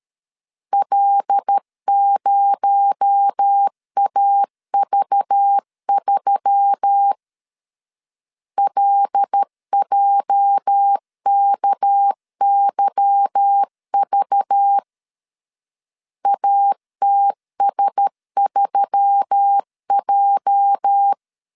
Morse code, heard as a keyed tone, about 13 wpm, 800 Hz. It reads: L0AV3 LJKYV ATS3J
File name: LEZIONI / 8 YJ45 / 8 YJ 12wpm 2235